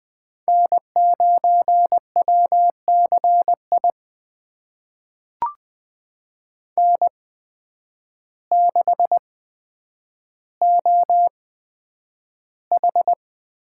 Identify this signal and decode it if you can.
Morse code, heard as a keyed tone, about 20 wpm, spaced as for 9 wpm, 700 Hz. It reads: N9WCI N6OH
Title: USA Call Signs - 8x Character Spacing - 20wpm